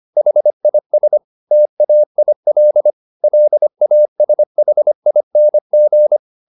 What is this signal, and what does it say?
Morse code, heard as a keyed tone, about 25 wpm, 600 Hz. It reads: HIS TAIL LASHING